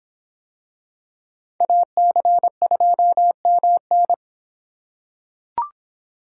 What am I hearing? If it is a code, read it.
Morse code, heard as a keyed tone, about 26 wpm, 700 Hz. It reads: AC2MN